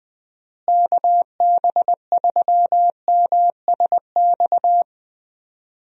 Morse code, heard as a keyed tone, about 20 wpm, 700 Hz. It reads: KB3MSX
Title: USA Call Signs - 6 Characters - Repeated 3x - 20wpm